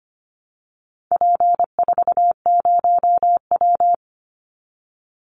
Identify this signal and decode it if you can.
Morse code, heard as a keyed tone, about 25 wpm, 700 Hz. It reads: P40W
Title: Call Signs Repeated 3x - 25wpm